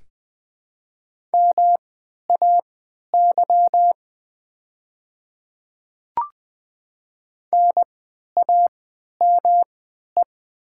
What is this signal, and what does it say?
Morse code, heard as a keyed tone, about 20 wpm, spaced as for 11 wpm, 700 Hz. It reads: MAY NAME
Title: 3x Character Spacing - Top 200 Words - 20wpm